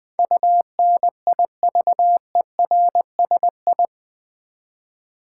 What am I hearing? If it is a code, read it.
Morse code, heard as a keyed tone, about 21 wpm, 700 Hz. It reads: UNIVERSI
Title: Build A Word - 4 Letters At A Time - 10-Letter Words - 20wpm